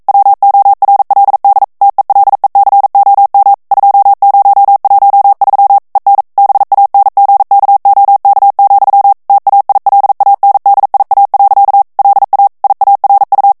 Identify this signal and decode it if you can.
Morse code, heard as a keyed tone, about 42 wpm, 800 Hz. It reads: WORLD TELECOM 2013 EN BANGKOK, TAILANDIA. LA IARU
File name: morse mp3 / ESPAGNOL TEXTE EN CLAIR 1 VITESSE 40 MOTS MINUTES